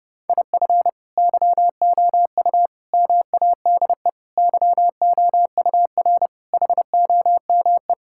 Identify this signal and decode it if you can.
Morse code, heard as a keyed tone, about 32 wpm, 700 Hz. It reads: IF YOU MADE YOUR HOME
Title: Sentences from Top 1600 Words 30wpm